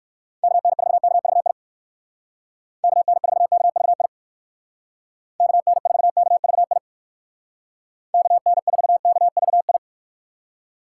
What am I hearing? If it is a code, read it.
Morse code, heard as a keyed tone, about 55 wpm, 700 Hz. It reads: KN4KVI KN4KVI KN4KVI KN4KVI
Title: Speed Racing - USA 2x3 Call Signs 45wpm